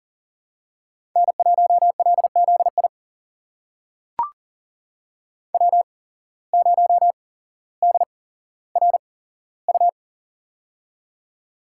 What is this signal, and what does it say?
Morse code, heard as a keyed tone, about 40 wpm, spaced as for 17 wpm, 700 Hz. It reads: N1LZI W0DRU
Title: USA Call Signs - 8x Character Spacing - 40wpm